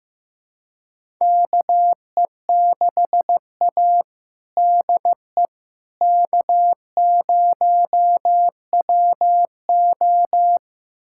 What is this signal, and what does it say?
Morse code, heard as a keyed tone, about 15 wpm, 700 Hz. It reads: KE6A DE K0WO
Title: USA Callsign DE Callsign - 4 Characters - 15wpm